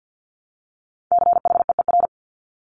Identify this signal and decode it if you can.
Morse code, heard as a keyed tone, about 50 wpm, 700 Hz. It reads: CHEER
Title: Multi-letter Phonemes Repeated 3x 50wpm